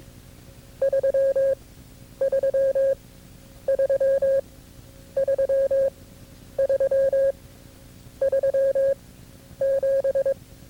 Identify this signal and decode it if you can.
Morse code, heard as a keyed tone, about 22 wpm, spaced as for 10 wpm, 540 Hz. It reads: 333 33 37
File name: CW-practice/KB6MT-tape2-side1